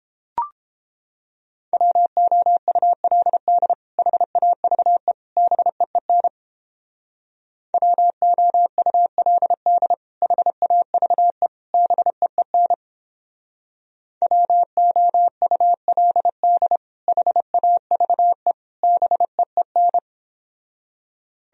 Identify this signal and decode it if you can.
Morse code, heard as a keyed tone, about 29 wpm, 700 Hz. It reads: WOULD HAVE BEEN WOULD HAVE BEEN WOULD HAVE BEEN